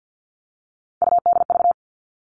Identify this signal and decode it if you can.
Morse code, heard as a keyed tone, about 50 wpm, 700 Hz. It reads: UDV
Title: All Cut Numbers - 3-Digit 50wpm